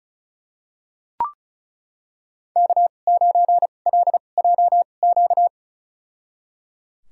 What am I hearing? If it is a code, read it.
Morse code, heard as a keyed tone, about 36 wpm, spaced as for 25 wpm, 700 Hz. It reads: K9LJQ
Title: USA Call Signs - 5 Characters - 2x Character Spacing - 35wpm